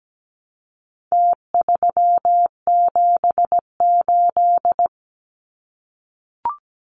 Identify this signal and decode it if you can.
Morse code, heard as a keyed tone, about 17 wpm, 700 Hz. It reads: T378
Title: Numbers - Contest Style - 4-Digit 17wpm